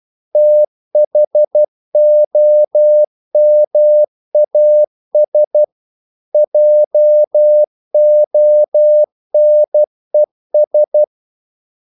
Morse code, heard as a keyed tone, about 12 wpm, 600 Hz. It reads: THOMAS JONES
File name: CW / Practice / EnglishQuotations0312